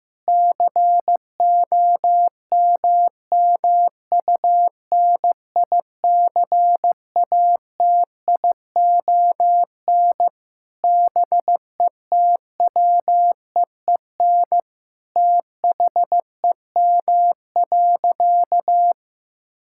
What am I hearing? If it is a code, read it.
Morse code, heard as a keyed tone, about 15 wpm, 700 Hz. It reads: COMMUNICATION BETWEEN THEM.
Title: Sentences from Top 1000 Words - Review Part 6 - 15wpm